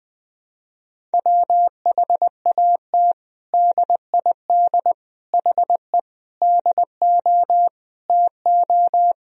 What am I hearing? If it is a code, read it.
Morse code, heard as a keyed tone, about 20 wpm, 700 Hz. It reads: WHAT DID HE DO TO
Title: Sentences from Top 100 Words and Abbreviations 20wpm